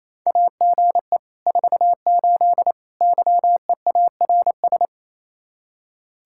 Morse code, heard as a keyed tone, about 36 wpm, 700 Hz. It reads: AGE 48 YEARS